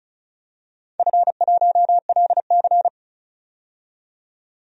Morse code, heard as a keyed tone, about 34 wpm, 700 Hz. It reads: F1LC
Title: DXCC-100 Call Signs - 4 Characters - 35wpm